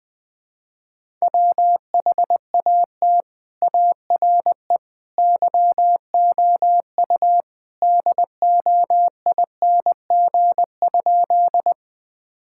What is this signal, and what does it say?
Morse code, heard as a keyed tone, about 20 wpm, 700 Hz. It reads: WHAT ARE YOU DOING?
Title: Sentences with 3-Word Phrases - Spoken Beginning - 20wpm